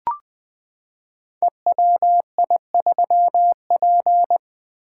Morse code, heard as a keyed tone, about 20 wpm, 700 Hz. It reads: EWI3P
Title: Code Groups - 5 Letter:Numbers - 20wpm